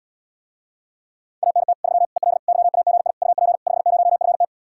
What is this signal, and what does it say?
Morse code, heard as a keyed tone, about 80 wpm, 700 Hz. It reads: ANT 3 EL YAGI UP 50FT